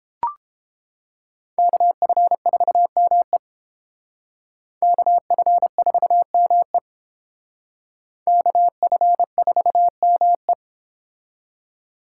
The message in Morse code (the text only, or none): KF4ME KF4ME KF4ME